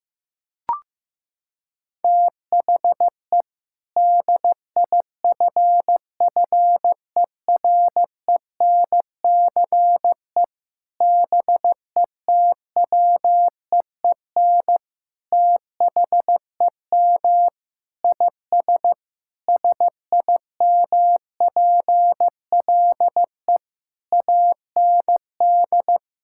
Morse code, heard as a keyed tone, about 16 wpm, 700 Hz. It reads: THE DIFFERENCE BETWEEN THEM IS SIMPLE AND
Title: Sentences from Top 1600 Words 15wpm